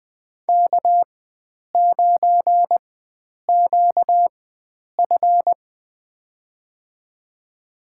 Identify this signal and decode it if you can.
Morse code, heard as a keyed tone, about 21 wpm, spaced as for 9 wpm, 700 Hz. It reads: K9QF
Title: USA Call Signs - 4 Characters - 4x Character Spacing - 20wpm